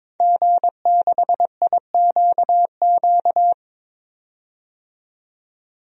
Morse code, heard as a keyed tone, about 22 wpm, 700 Hz. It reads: G6IQQ
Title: DXCC-100 Call Signs - 5 Characters - 22wpm